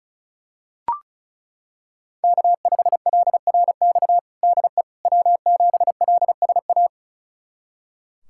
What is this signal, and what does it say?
Morse code, heard as a keyed tone, about 36 wpm, 700 Hz. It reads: K5LRX DE W7LSA
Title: USA Callsign DE Callsign 35wpm